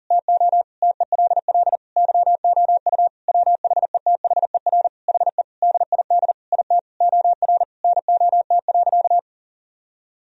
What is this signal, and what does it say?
Morse code, heard as a keyed tone, about 40 wpm, 700 Hz. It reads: TO TELL YOU WHETHER HE DID IT OR NOT.
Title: Sentences from Top 1000 Words - Review Part 3 - 40wpm